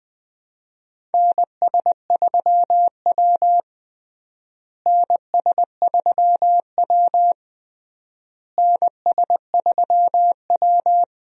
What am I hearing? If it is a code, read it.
Morse code, heard as a keyed tone, about 21 wpm, 700 Hz. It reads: NS3W NS3W NS3W
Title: USA Call Signs - 4 Characters - Repeated 3x - 20wpm